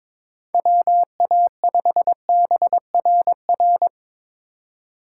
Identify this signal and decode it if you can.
Morse code, heard as a keyed tone, about 22 wpm, 700 Hz. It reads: WA5BRR
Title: USA Call Signs - 6 Characters - 4x Character Spacing - 22wpm